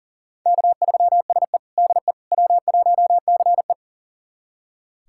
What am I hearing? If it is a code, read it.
Morse code, heard as a keyed tone, about 40 wpm, 700 Hz. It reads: K3SE DE W1CE
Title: USA Callsign DE Callsign - 4 Characters - 40wpm